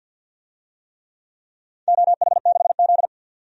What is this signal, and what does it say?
Morse code, heard as a keyed tone, about 51 wpm, 700 Hz. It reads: OH6Z
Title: DXCC-100 Call Signs - 4 Characters - 50wpm